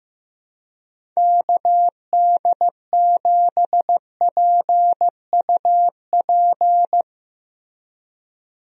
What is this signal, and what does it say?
Morse code, heard as a keyed tone, about 15 wpm, 700 Hz. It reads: KD7PUP
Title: USA Call Signs - 6 Characters - Repeated 2x - 15wpm